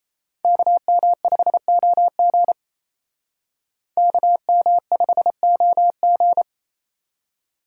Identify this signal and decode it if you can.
Morse code, heard as a keyed tone, about 34 wpm, 700 Hz. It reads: KM5OG KM5OG
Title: Speed Racing - USA 2x2 Call Signs 28wpm